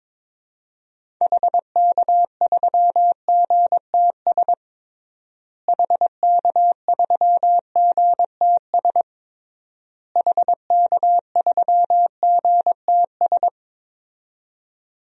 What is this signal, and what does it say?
Morse code, heard as a keyed tone, about 22 wpm, 700 Hz. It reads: HK3GTS HK3GTS HK3GTS
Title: DXCC-100 Call Signs - 6 Characters - Repeated 3x - 22wpm